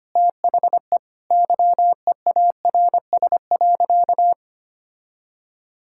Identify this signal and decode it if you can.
Morse code, heard as a keyed tone, about 25 wpm, 700 Hz. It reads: THE YEARS.
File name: practice / 25wpm / Sentences with 3-Word Phrases - 25wpm